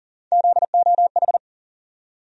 Morse code, heard as a keyed tone, about 40 wpm, 700 Hz. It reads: ZOH
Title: USA Call Sign Postfixes - 3 Letters 40wpm